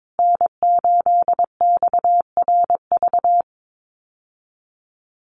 Morse code, heard as a keyed tone, about 22 wpm, 700 Hz. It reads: N8XRV
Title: USA Call Signs Repeated 3x 22wpm